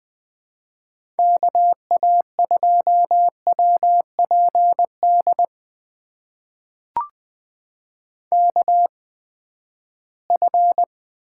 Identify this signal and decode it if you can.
Morse code, heard as a keyed tone, about 20 wpm, spaced as for 14 wpm, 700 Hz. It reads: KA2WPD KF
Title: USA Call Signs - 6 Characters - 8x Character Spacing - 20wpm